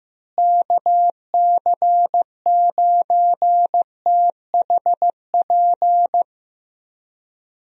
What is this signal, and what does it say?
Morse code, heard as a keyed tone, about 15 wpm, 700 Hz. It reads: KC9THP
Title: USA Call Signs - 6 Characters - Repeated 2x - 15wpm